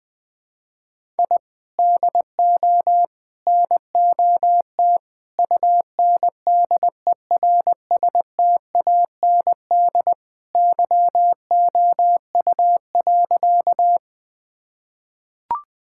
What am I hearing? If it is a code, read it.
Morse code, heard as a keyed tone, about 20 wpm, 700 Hz. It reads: I DO NOT UNDERSTAND YOU.
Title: Sentences with 4-Word Phrases - 20wpm